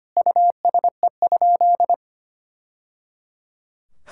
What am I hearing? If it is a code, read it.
Morse code, heard as a keyed tone, about 25 wpm, 700 Hz. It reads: USE?
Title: Sentences from Top 1000 Words - Review Part 6 - 25wpm